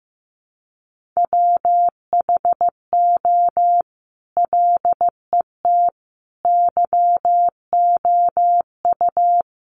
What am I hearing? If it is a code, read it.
Morse code, heard as a keyed tone, about 15 wpm, 700 Hz. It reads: WHO LET YOU
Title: Sentences from Top 300 Words 15wpm